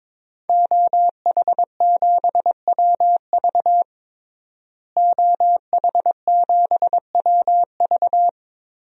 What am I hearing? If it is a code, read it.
Morse code, heard as a keyed tone, about 22 wpm, 700 Hz. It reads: OH7WV OH7WV
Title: DXCC-100 Call Signs - 5 Characters - Repeated 3x - 22wpm